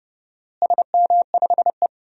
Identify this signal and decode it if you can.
Morse code, heard as a keyed tone, about 30 wpm, 700 Hz. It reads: SM5E
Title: DXCC-100 Call Signs - 4 Characters - Repeated 2x - 30wpm